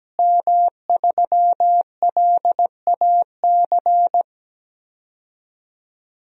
Morse code, heard as a keyed tone, about 17 wpm, 700 Hz. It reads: M3LAC
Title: DXCC-100 Call Signs - 5 Characters - Repeated 3x - 17wpm